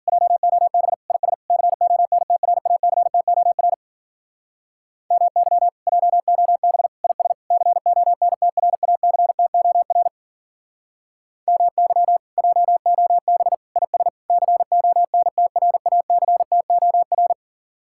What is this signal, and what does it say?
Morse code, heard as a keyed tone, about 52 wpm, 700 Hz. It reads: JOB IS CONTRACTOR MY JOB IS CONTRACTOR MY JOB IS CONTRACTOR